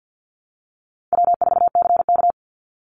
Morse code, heard as a keyed tone, about 50 wpm, 700 Hz. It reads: W4CK
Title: Easy Call Signs Repeated 2x - 50wpm